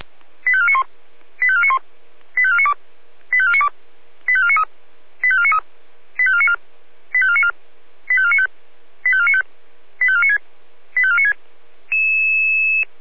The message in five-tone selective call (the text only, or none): 85491 85491 85492 85492 85493 85493 85494 85494 85495 85495 85496 85496